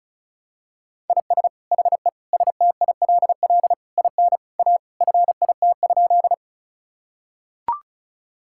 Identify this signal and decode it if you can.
Morse code, heard as a keyed tone, about 36 wpm, 700 Hz. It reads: IS HE STILL IN A FIT?
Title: Sentences from Top 1000 Words - Review Part 5 - 35wpm